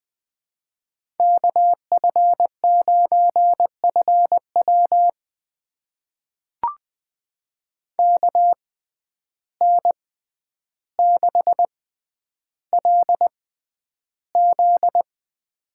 Morse code, heard as a keyed tone, about 20 wpm, spaced as for 10 wpm, 700 Hz. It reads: KF9FW KN6LZ
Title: USA Call Signs - 5 Characters - 6x Character Spacing - 20wpm